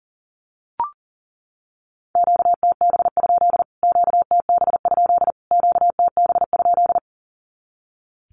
Morse code, heard as a keyed tone, about 40 wpm, 700 Hz. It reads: QTB? QTB? QTB?
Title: Q-Codes Repeated 3x 40wpm